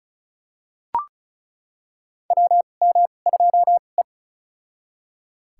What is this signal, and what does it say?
Morse code, heard as a keyed tone, about 36 wpm, spaced as for 26 wpm, 700 Hz. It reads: WM2E
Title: USA Call Signs - 4 Characters - 2x Character Spacing - 35wpm